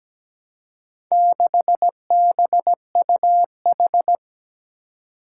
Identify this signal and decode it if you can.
Morse code, heard as a keyed tone, about 17 wpm, 700 Hz. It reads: 6BUH
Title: Four Challenging Letter-Number Combinations 17wpm